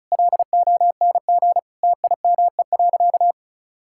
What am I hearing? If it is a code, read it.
Morse code, heard as a keyed tone, about 34 wpm, 700 Hz. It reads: LONG TIME.
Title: Sentences from Top 1600 Words 35wpm